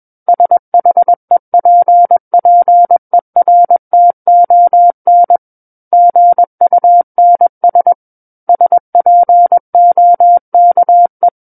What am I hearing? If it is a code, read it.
Morse code, heard as a keyed tone, about 21 wpm, 700 Hz. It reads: SHEPPERTON GUNS SPOKE
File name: Book One / 12-Chapter 12 - 21 WPM